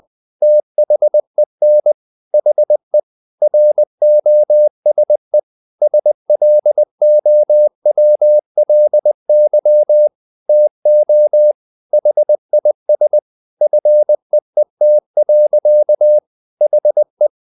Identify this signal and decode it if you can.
Morse code, heard as a keyed tone, about 20 wpm, 600 Hz. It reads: THEN HE ROSE SLOWLY TO HIS FEET. HE
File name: the old man and the sea part 2 20wpm